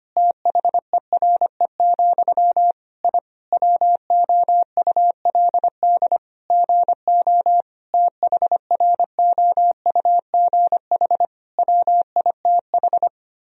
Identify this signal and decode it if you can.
Morse code, heard as a keyed tone, about 25 wpm, 700 Hz. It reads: THERE, I WOULD GO THROUGH WITH
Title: Sentences from Top 1000 Words - Review Part 3 - 25wpm